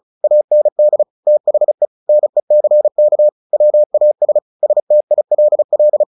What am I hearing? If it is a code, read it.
Morse code, heard as a keyed tone, about 36 wpm, 600 Hz. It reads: AND THE NECK WAS STILL
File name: the old man and the sea / the old man and the sea 1 35wpm